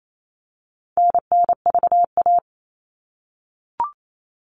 Morse code, heard as a keyed tone, about 28 wpm, 700 Hz. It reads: NNVA
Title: All Cut Numbers - 4-Digit 28wpm